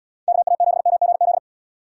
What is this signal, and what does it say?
Morse code, heard as a keyed tone, about 76 wpm, 700 Hz. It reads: KA7MGZ